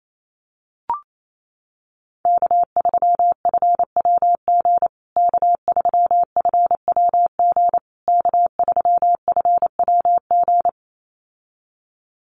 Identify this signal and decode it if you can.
Morse code, heard as a keyed tone, about 28 wpm, 700 Hz. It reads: K3FWG K3FWG K3FWG
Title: USA Call Signs Repeated 3x 28wpm